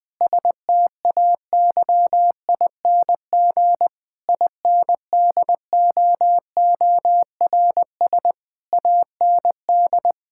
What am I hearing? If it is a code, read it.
Morse code, heard as a keyed tone, about 20 wpm, 700 Hz. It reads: STAYING INDOORS AND
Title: Difficult Sentences with Hints - 20wpm